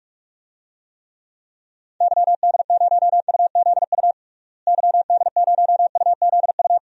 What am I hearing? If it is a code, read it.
Morse code, heard as a keyed tone, about 46 wpm, 700 Hz. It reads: YD0UZU YD0UZU
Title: DXCC-100 Call Signs - 6 Characters - Repeated 3x - 45wpm